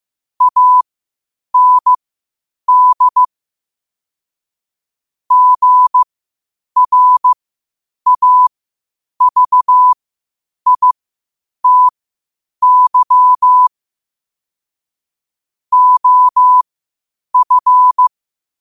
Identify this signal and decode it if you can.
Morse code, heard as a keyed tone, about 15 wpm, spaced as for 8 wpm, 1000 Hz. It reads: AND GRAVITY OF
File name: qotdfiles / 8wpm